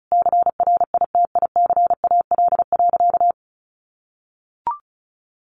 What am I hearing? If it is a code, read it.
Morse code, heard as a keyed tone, about 34 wpm, 700 Hz. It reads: CRITICAL.
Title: Sentences from Top 1100 Words 35wpm